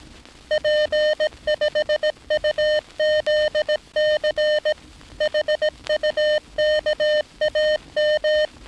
Morse code, heard as a keyed tone, about 18 wpm, 610 Hz. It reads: P5UZC HUKAM
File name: training / NAVPERS 11049to11060 / NAVPERS 11059 RA RB